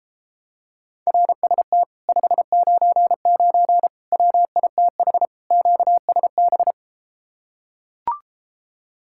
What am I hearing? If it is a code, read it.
Morse code, heard as a keyed tone, about 33 wpm, 700 Hz. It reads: RST 599 WITH QSB